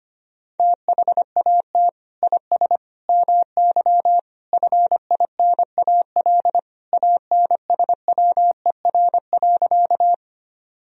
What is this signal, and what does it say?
Morse code, heard as a keyed tone, about 25 wpm, 700 Hz. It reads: THAT IS MY FINAL ANSWER.
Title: Sentences from Top 1000 Words - Review Part 2 - 25wpm